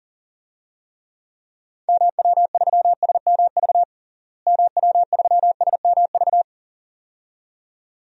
Morse code, heard as a keyed tone, about 40 wpm, 700 Hz. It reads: MW3SMV MW3SMV
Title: DXCC-100 Call Signs - 6 Characters - Repeated 2x - 40wpm